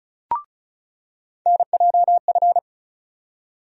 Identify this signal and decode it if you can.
Morse code, heard as a keyed tone, about 34 wpm, 700 Hz. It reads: NJF
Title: USA Call Sign Postfixes - 3 Letters 35wpm